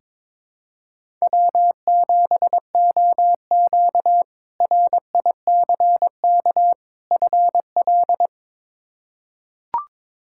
Morse code, heard as a keyed tone, about 22 wpm, 700 Hz. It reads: W7OQ RICK FL
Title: K1USN SST QSO Practice - 22wpm